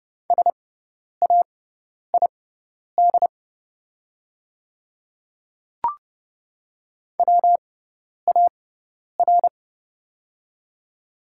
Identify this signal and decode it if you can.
Morse code, heard as a keyed tone, about 30 wpm, spaced as for 10 wpm, 700 Hz. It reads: SAID WAR
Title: 6x Character Spacing - Top 500 Words - 30wpm